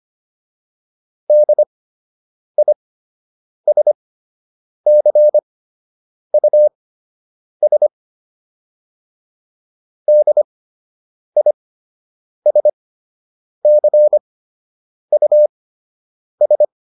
Morse code, heard as a keyed tone, about 25 wpm, spaced as for 8 wpm, 600 Hz. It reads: DISCUS DISCUS
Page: fundamental-practice-files